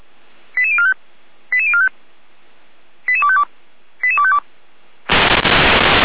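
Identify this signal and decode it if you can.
Five-tone selective call (ZVEI), heard as quick runs of five tones, tones of about 80 ms. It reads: 88045 88045 88252 88252
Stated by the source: ZVEI five-tone call 88447 received with an SDR, archived by Feuerwehr Orlamünde.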